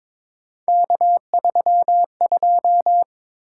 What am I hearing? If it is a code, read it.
Morse code, heard as a keyed tone, about 23 wpm, 700 Hz. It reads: K32
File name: practice / 22wpm / Three Letter-Number Combinations - 22wpm